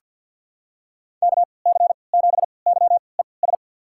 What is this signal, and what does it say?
Morse code, heard as a keyed tone, about 53 wpm, spaced as for 29 wpm, 700 Hz. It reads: KC7YES
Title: USA Call Signs - 6 Characters - 3x Character Spacing - 50wpm